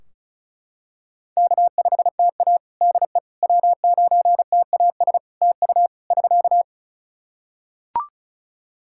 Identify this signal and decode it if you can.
Morse code, heard as a keyed tone, about 34 wpm, 700 Hz. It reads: K5TA DE W9TAS TU <SK>
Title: Practice QSO - Exchange 4 - Ending - 35wpm